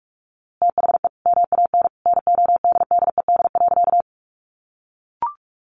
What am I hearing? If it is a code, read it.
Morse code, heard as a keyed tone, about 44 wpm, 700 Hz. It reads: THE MAN NODDED.
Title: A Sound Of Thunder 45wpm